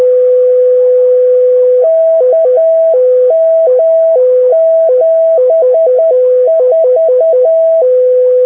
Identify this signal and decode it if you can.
Morse code, heard as a keyed tone, about 10 wpm, 650 Hz, with noise. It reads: KM74
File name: vhf / soundclips6m / od5six 6m